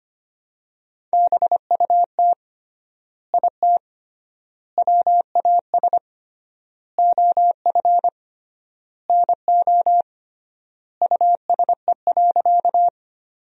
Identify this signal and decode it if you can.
Morse code, heard as a keyed tone, about 25 wpm, 700 Hz. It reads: BUT IT WAS OF NO USE.
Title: 3x Word-Spacing - Sentences from Top 100 Words - 25wpm